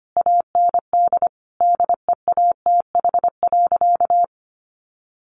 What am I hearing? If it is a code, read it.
Morse code, heard as a keyed tone, about 25 wpm, 700 Hz. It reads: AND DEATH.
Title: Sentences from Top 1100 Words 25wpm